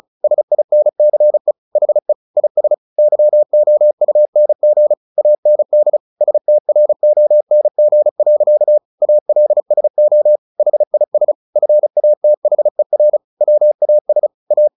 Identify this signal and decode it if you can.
Morse code, heard as a keyed tone, about 34 wpm, 600 Hz. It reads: SINCE HE IS YOUNG AND STRONG. ALSO HIS FATHER WAS A
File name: the old man and the sea / the old man and the sea 1 35wpm